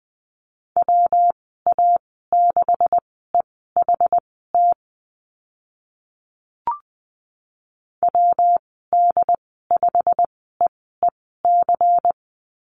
Morse code, heard as a keyed tone, about 20 wpm, spaced as for 14 wpm, 700 Hz. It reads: WA6EHT WD5EEC
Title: 140 Lesson 23g - Letter f - Call Signs 20wpm